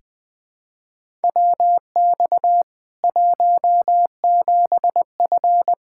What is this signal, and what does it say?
Morse code, heard as a keyed tone, about 21 wpm, 700 Hz. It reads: WX 17F
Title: K1USN SST QSO Practice 20wpm